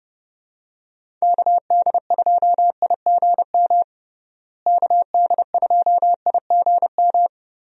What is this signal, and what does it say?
Morse code, heard as a keyed tone, about 30 wpm, 700 Hz. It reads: KD2IGM KD2IGM
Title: USA Call Signs - 6 Characters - Repeated 2x - 30wpm